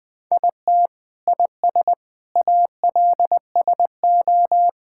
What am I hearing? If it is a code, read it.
Morse code, heard as a keyed tone, about 21 wpm, 700 Hz. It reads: IT IS ALSO
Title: Sets of 8 Words - 20wpm